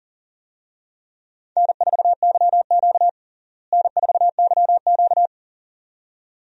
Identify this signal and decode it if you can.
Morse code, heard as a keyed tone, about 40 wpm, 700 Hz. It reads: N4YQ N4YQ
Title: USA Call Signs - 4 Characters - Repeated 2x - 40wpm